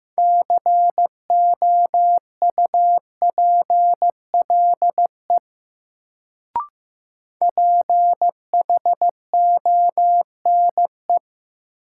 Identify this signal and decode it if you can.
Morse code, heard as a keyed tone, about 15 wpm, 700 Hz. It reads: COUPLE PHONE
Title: Sentences from Top 1100 Words - New Word Drill Alt - 15wpm